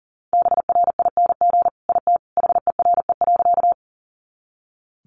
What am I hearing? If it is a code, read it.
Morse code, heard as a keyed tone, about 40 wpm, 700 Hz. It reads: BRING IT HERE.
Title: Sentences from Top 500 Words 40wpm